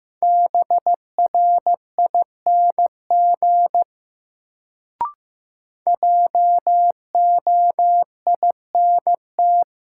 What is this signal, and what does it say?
Morse code, heard as a keyed tone, about 15 wpm, 700 Hz. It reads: BRING JOINT
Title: Sets of 5 Words Encore - New Word Drill - 15wpm